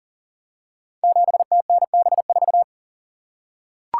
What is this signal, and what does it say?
Morse code, heard as a keyed tone, about 38 wpm, 700 Hz. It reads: 7TNB4
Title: Code Groups - 5 Letter:Numbers - 40wpm